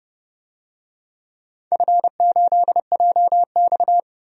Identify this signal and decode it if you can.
Morse code, heard as a keyed tone, about 31 wpm, 700 Hz. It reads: F8JX
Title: DXCC-100 Call Signs - 4 Characters - 30wpm